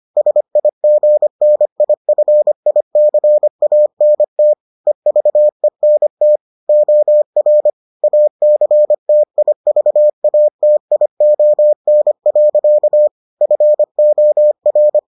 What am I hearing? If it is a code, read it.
Morse code, heard as a keyed tone, about 25 wpm, 600 Hz. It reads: SIGNIFICANT EVENT OR ACTIVATION. FOR